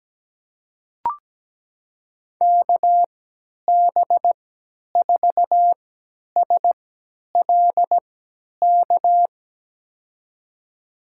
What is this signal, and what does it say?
Morse code, heard as a keyed tone, about 17 wpm, spaced as for 10 wpm, 700 Hz. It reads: KB4SLK
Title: USA Call Signs - 6 Characters - 3x Character Spacing - 17wpm